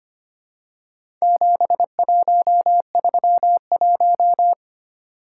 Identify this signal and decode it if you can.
Morse code, heard as a keyed tone, about 25 wpm, 700 Hz. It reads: 7131
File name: practice / 25wpm / Four Letter-Number Combinations - 25wpm